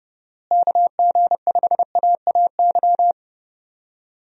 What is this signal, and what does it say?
Morse code, heard as a keyed tone, about 29 wpm, 700 Hz. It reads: KG5AAY
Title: USA Call Signs - 6 Characters - Repeated 3x - 30wpm